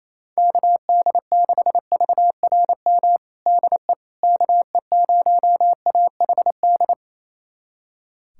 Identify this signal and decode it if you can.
Morse code, heard as a keyed tone, about 28 wpm, 700 Hz. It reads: KD6VRM DE KE0AHD
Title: USA Callsign DE Callsign 28wpm